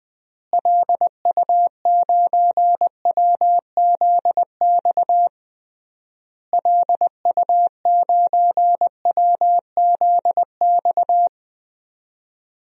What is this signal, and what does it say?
Morse code, heard as a keyed tone, about 21 wpm, 700 Hz. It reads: LU9WZX LU9WZX
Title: DXCC-100 Call Signs - 6 Characters - Repeated 3x - 20wpm